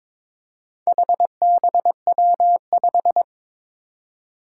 Morse code, heard as a keyed tone, about 22 wpm, 700 Hz. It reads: HBW5
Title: Four Challenging Letter-Number Combinations 22wpm